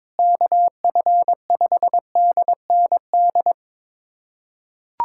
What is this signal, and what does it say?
Morse code, heard as a keyed tone, about 22 wpm, 700 Hz. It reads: KF5DND
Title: USA Callsign DE Callsign 22wpm